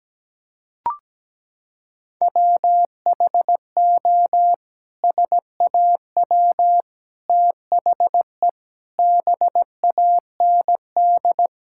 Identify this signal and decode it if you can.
Morse code, heard as a keyed tone, about 17 wpm, 700 Hz. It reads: WHO SAW THE BAND
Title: Sentences from Top 1600 Words 17wpm